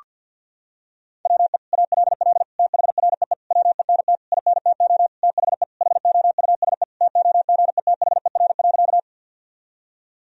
Morse code, heard as a keyed tone, about 51 wpm, 700 Hz. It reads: WE ALL THREE WENT INTO THE HOUSE TOGETHER.
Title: Sentences from Top 1000 Words - Review Part 1 - 50wpm